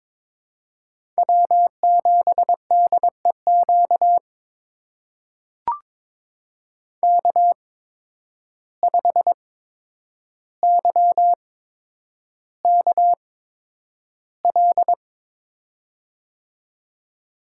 Morse code, heard as a keyed tone, about 22 wpm, spaced as for 9 wpm, 700 Hz. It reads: W7DEQ K5YKL
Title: USA Call Signs - 5 Characters - 8x Character Spacing - 22wpm